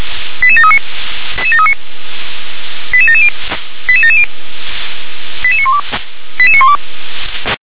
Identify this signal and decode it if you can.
Five-tone selective call (ZVEI), heard as quick runs of five tones, tones of about 70 ms. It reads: 88739 88770 88770 88912 88912
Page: /357/SDR/ZVEI/88737/